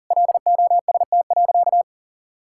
Morse code, heard as a keyed tone, about 40 wpm, 700 Hz. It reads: LOST.
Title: Sentences from Top 1600 Words 40wpm